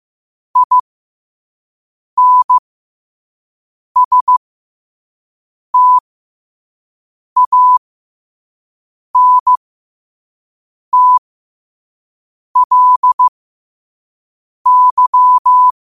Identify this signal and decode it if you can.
Morse code, heard as a keyed tone, about 15 wpm, spaced as for 5 wpm, 1000 Hz. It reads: INSTANTLY
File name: qotdfiles / 5wpm